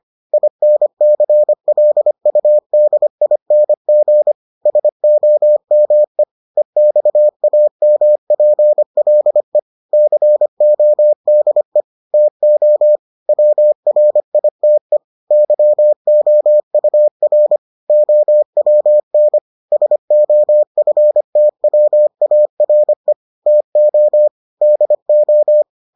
Morse code, heard as a keyed tone, about 25 wpm, 600 Hz. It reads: INCLUDING SOME EXAMPLE CODE TO WRITE YOUR OWN SOFTWARE TO DO